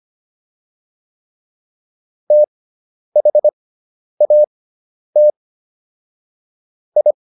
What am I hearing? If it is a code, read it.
Morse code, heard as a keyed tone, about 25 wpm, spaced as for 10 wpm, 600 Hz. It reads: THAT I